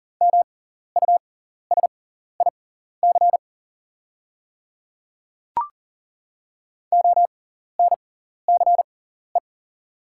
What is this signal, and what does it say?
Morse code, heard as a keyed tone, about 40 wpm, spaced as for 14 wpm, 700 Hz. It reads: MUSIC ONCE
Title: 6x Character Spacing - Top 300 Words - 40wpm